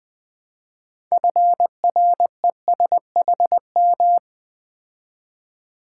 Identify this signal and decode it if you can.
Morse code, heard as a keyed tone, about 20 wpm, 700 Hz. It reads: FRESHM
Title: Build A Word - 2 Letters At A Time - 8-Letter Words - 20wpm